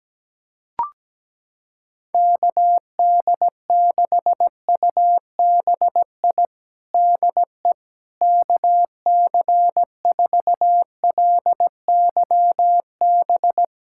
Morse code, heard as a keyed tone, about 17 wpm, 700 Hz. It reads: KD6UBI DE KC4LYB
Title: USA Callsign DE Callsign - 6 Characters - 17wpm